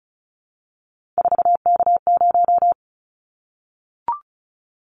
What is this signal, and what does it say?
Morse code, heard as a keyed tone, about 36 wpm, 700 Hz. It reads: 4K0
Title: Callsign Prefixes for Top 100 DXCC - 35wpm